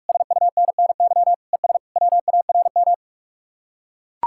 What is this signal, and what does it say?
Morse code, heard as a keyed tone, about 48 wpm, 700 Hz. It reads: SUNNY ES WARM